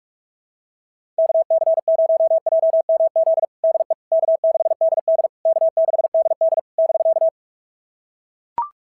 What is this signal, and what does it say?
Morse code, heard as a keyed tone, about 42 wpm, 690 Hz. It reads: KC0JMZ DE K6DD K6DD <BK>
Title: SOTA QSO Practice 45wpm